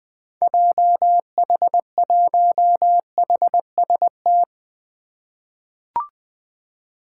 Morse code, heard as a keyed tone, about 21 wpm, 700 Hz. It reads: JH1HST
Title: 2023 POTA Activator Call Signs - 20wpm